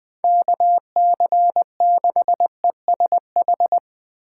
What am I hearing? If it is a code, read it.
Morse code, heard as a keyed tone, about 22 wpm, 700 Hz. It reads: KC6ESH
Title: Speed Racing - USA Call Signs 20wpm